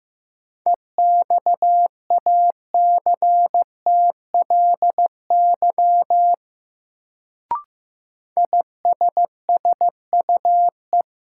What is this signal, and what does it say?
Morse code, heard as a keyed tone, about 15 wpm, 700 Hz. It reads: EXACTLY ISSUE
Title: Sentences from Top 900 Words - New Word Drill - 15wpm